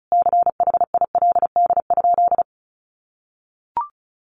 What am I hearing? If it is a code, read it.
Morse code, heard as a keyed tone, about 34 wpm, 700 Hz. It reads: CHILD?
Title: Sentences from Top 1000 Words 35wpm